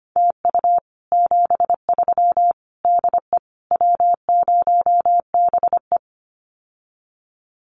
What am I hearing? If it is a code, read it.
Morse code, heard as a keyed tone, about 25 wpm, 700 Hz. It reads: TU 73 DE W0BE
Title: Common QSO Phrases Encore 25wpm